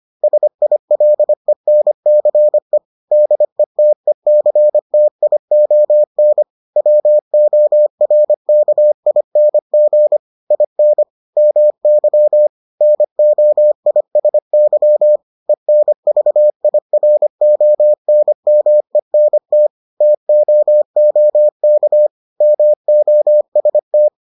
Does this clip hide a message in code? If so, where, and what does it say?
Yes: Morse code from 0.2 to 24.1 s, heard as a keyed tone, about 25 wpm, 600 Hz. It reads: SILENCE DETECTION WORKING IN MY NOISY ENVIRONMENT TOOK MOST